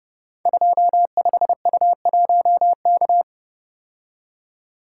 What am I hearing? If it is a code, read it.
Morse code, heard as a keyed tone, about 30 wpm, 700 Hz. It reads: 25U1K
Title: Code Groups - 5 Letter:Numbers - 30wpm